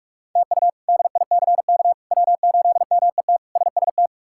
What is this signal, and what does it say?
Morse code, heard as a keyed tone, about 46 wpm, 700 Hz. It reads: TU DICK W8MET SST
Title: K1USN SST QSO Practice - 45wpm